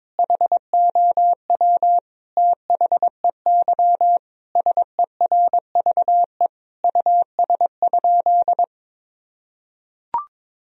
Morse code, heard as a keyed tone, about 22 wpm, 700 Hz. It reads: HOW THEY SERVE US?
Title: Sentences from Top 1000 Words - Review Part 2 - 22wpm